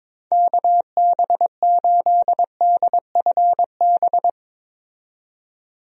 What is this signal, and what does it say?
Morse code, heard as a keyed tone, about 21 wpm, 700 Hz. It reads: KB8DFB
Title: USA Call Signs - 6 Characters - 22wpm